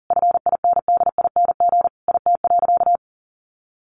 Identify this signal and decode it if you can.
Morse code, heard as a keyed tone, about 40 wpm, 700 Hz. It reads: FINDING IT.
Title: Sentences from Top 1100 Words 40wpm